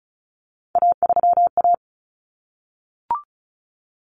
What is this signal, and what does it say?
Morse code, heard as a keyed tone, about 34 wpm, 700 Hz. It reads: A3A
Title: Common Cut Numbers - 3-Digit 35wpm